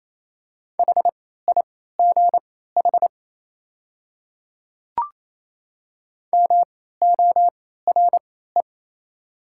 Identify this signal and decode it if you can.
Morse code, heard as a keyed tone, about 27 wpm, spaced as for 16 wpm, 700 Hz. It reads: HIGH MORE
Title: 3x Character Spacing - Top 300 Words - 28wpm